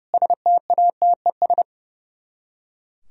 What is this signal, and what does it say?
Morse code, heard as a keyed tone, about 36 wpm, 700 Hz. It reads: STATES
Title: Speed-Racing - Sets of 4 Words 30wpm